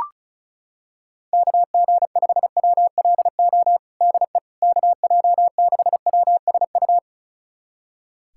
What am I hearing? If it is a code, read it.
Morse code, heard as a keyed tone, about 34 wpm, 700 Hz. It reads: KG5WLO DE KJ6WSU
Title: USA Callsign DE Callsign 35wpm